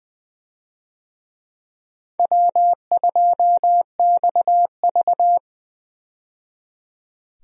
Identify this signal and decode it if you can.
Morse code, heard as a keyed tone, about 20 wpm, 700 Hz. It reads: W2XV
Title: USA Call Signs - 4 Characters - 20wpm